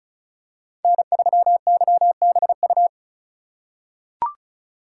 Morse code, heard as a keyed tone, about 34 wpm, 700 Hz. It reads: N3YBU
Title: USA Call Signs - 5 Characters - 2x Character Spacing - 35wpm